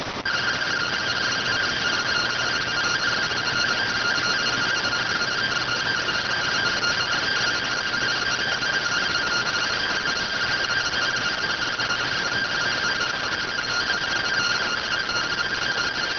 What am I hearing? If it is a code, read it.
Two-tone teleprinter signal, about 170 Hz shift.